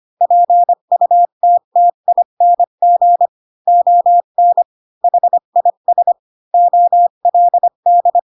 Morse code, heard as a keyed tone, about 25 wpm, 700 Hz. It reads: PUTTING ON HIS OLD
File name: Pinocchio / Pinocchio ch8 20 25 ik1hge